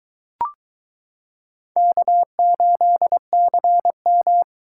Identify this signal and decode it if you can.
Morse code, heard as a keyed tone, about 22 wpm, 700 Hz. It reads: K8CM